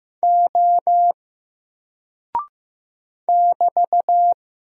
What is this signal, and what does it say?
Morse code, heard as a keyed tone, about 15 wpm, 700 Hz. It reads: O <BT>
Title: Single Character (Letter:Number:Punctuation:Prosign) Alt 15wpm